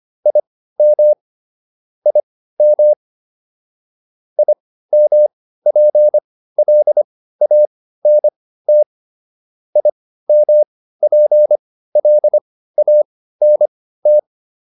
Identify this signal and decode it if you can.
Morse code, heard as a keyed tone, about 25 wpm, spaced as for 15 wpm, 600 Hz. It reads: IM IM IMPLANT IMPLANT